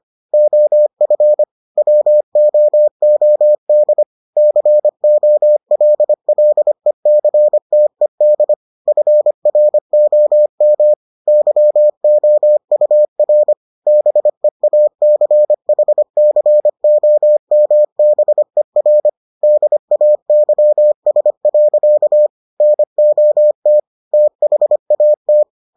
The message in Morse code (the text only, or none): OF WOOD COLLECTED FROM YOUR BEACHCOMBER DAYS. NOT THAT